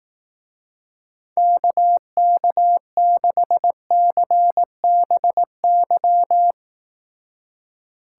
Morse code, heard as a keyed tone, about 18 wpm, 700 Hz. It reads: KK6CBY
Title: Speed Racing - USA 2x3 Call Signs 15wpm